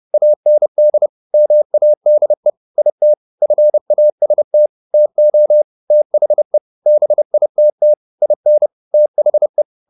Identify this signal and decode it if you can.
Morse code, heard as a keyed tone, about 30 wpm, 600 Hz. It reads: AND MADE IT FAST TO THE BITT IN THE